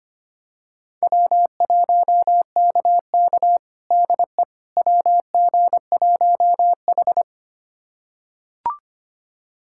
Morse code, heard as a keyed tone, about 25 wpm, 700 Hz. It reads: W1KK DE WG1H
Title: USA Callsign DE Callsign - 4 Characters - 25wpm